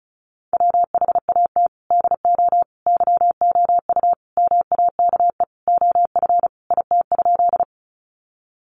Morse code, heard as a keyed tone, about 34 wpm, 700 Hz. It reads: WHAT DO YOU MAKE OF IT?
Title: Sentences from Top 100 Words 35wpm